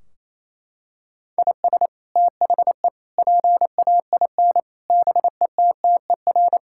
Morse code, heard as a keyed tone, about 28 wpm, 700 Hz. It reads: IS THE PAIN BETTER
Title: Sentences from Top 1600 Words 28wpm